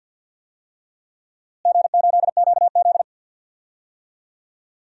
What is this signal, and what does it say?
Morse code, heard as a keyed tone, about 50 wpm, 700 Hz. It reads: G8QZ
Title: DXCC-100 Call Signs - 4 Characters - Repeated 3x - 50wpm